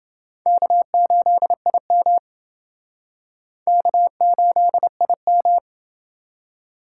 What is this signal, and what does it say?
Morse code, heard as a keyed tone, about 28 wpm, 700 Hz. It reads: K8IM K8IM